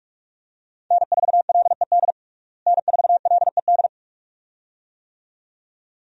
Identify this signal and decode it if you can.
Morse code, heard as a keyed tone, about 46 wpm, 700 Hz. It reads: N4LED N4LED
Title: USA Call Signs - 5 Characters - Repeated 2x - 45wpm